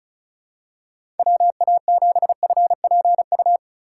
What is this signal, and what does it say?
Morse code, heard as a keyed tone, about 36 wpm, 700 Hz. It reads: WA7FPU
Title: USA Call Signs - 6 Characters - 4x Character Spacing - 35wpm